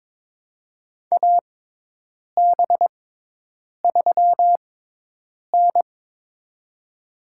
Morse code, heard as a keyed tone, about 22 wpm, spaced as for 8 wpm, 700 Hz. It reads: AB3N
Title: USA Call Signs - 6x Character Spacing - 22wpm